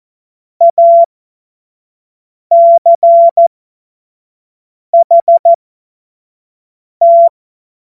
Morse code, heard as a keyed tone, about 14 wpm, spaced as for 5 wpm, 680 Hz. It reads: ACHT
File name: Morsekurs / Witze / Witz 03 Juni 2025 25 F70